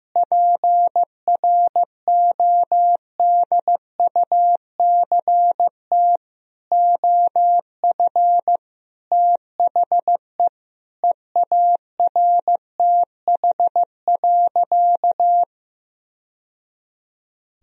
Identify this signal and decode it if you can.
Morse code, heard as a keyed tone, about 15 wpm, 700 Hz. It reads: PRODUCT OF THE EARTH.
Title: Sentences from Top 1000 Words - Review Part 2 - 15wpm